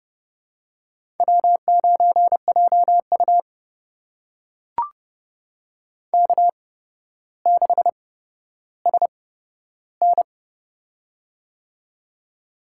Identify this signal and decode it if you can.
Morse code, heard as a keyed tone, about 30 wpm, spaced as for 13 wpm, 700 Hz. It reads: W9JU K6SN
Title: USA Call Signs - 4 Characters - 8x Character Spacing - 30wpm